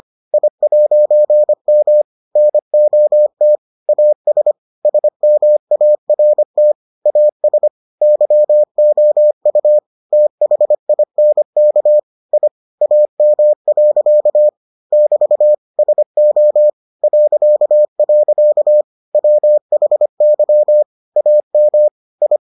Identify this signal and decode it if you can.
Morse code, heard as a keyed tone, about 25 wpm, 600 Hz. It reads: I'M NOT AS SMART AS YOU THINK I AM. <BT> SO .. WHY AM I